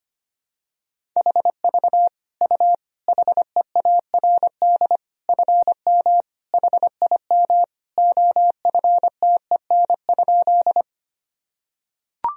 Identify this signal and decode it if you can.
Morse code, heard as a keyed tone, about 25 wpm, 700 Hz. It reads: HV U HEARD FM HIM OFTEN?
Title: Sentences from Top 400 Words and Abbreviations 25wpm